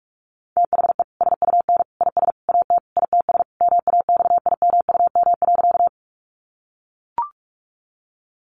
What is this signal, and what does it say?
Morse code, heard as a keyed tone, about 44 wpm, 700 Hz. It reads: THE SUN IS AT ITS MAXIMUM.
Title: Sentences from Top 1300 Words 45wpm